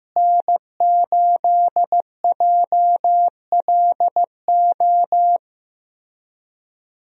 Morse code, heard as a keyed tone, about 16 wpm, 700 Hz. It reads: N8JLO
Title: USA Call Signs - 5 Characters - Repeated 3x - 15wpm